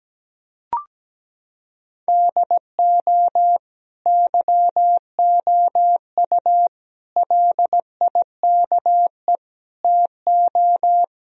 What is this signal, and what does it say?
Morse code, heard as a keyed tone, about 17 wpm, 700 Hz. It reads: DO YOU LIKE TO
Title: Sentences from Top 1000 Words - Review Part 1 - 17wpm